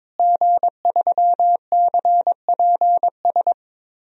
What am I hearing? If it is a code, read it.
Morse code, heard as a keyed tone, about 22 wpm, 700 Hz. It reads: G3CPS
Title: DXCC-100 Call Signs - 5 Characters - Repeated 2x - 22wpm